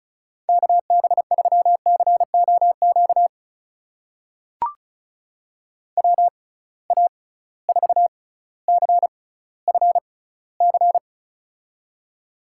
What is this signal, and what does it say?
Morse code, heard as a keyed tone, about 34 wpm, spaced as for 18 wpm, 700 Hz. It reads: KB3COQ WA4CFC
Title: USA Call Signs - 6 Characters - 6x Character Spacing - 35wpm